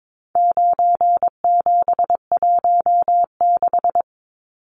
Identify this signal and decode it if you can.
Morse code, heard as a keyed tone, about 22 wpm, 700 Hz. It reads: 9716
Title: Numbers - 4-Digit 22wpm